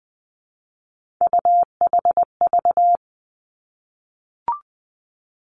Morse code, heard as a keyed tone, about 20 wpm, 700 Hz. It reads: UHV
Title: Three Challenging Letter-Number Combinations 20wpm